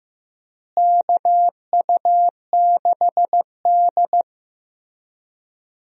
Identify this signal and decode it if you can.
Morse code, heard as a keyed tone, about 15 wpm, 700 Hz. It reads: KU6D
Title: USA Call Signs - 4 Characters - Repeated 3x - 15wpm